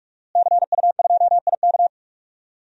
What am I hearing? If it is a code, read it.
Morse code, heard as a keyed tone, about 45 wpm, 700 Hz. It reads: CU2IK
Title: DXCC-100 Call Signs - 5 Characters - Repeated 2x - 45wpm